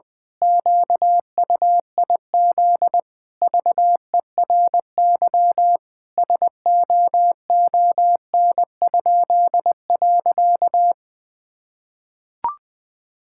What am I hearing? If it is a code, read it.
Morse code, heard as a keyed tone, about 20 wpm, 700 Hz. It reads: QUIZ VERY SOON?.
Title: Pangram Challenge v2 - 20wpm